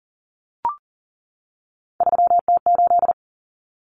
Morse code, heard as a keyed tone, about 40 wpm, 700 Hz. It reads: 3T8
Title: Common Cut Numbers - 3-Digit 40wpm